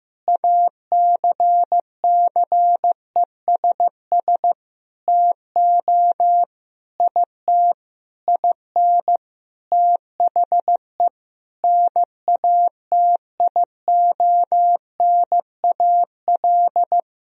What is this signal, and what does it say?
Morse code, heard as a keyed tone, about 15 wpm, 700 Hz. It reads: ACCESS TO IT IN THE NATIONAL
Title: Sentences from Top 1000 Words - Review Part 4 - 15wpm